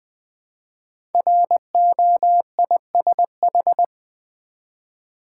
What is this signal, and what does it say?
Morse code, heard as a keyed tone, about 20 wpm, 700 Hz. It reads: ROISH
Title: Code Groups - 5 Letters - 20wpm